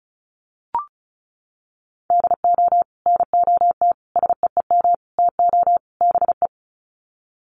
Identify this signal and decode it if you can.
Morse code, heard as a keyed tone, about 34 wpm, 700 Hz. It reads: DO NOT SEEM TO BE
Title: Sets of 5 Words Encore2 35wpm